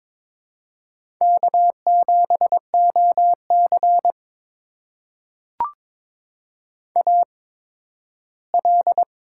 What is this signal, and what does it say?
Morse code, heard as a keyed tone, about 22 wpm, spaced as for 13 wpm, 700 Hz. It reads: K7OC AL
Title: USA Call Signs - 4 Characters - 8x Character Spacing - 22wpm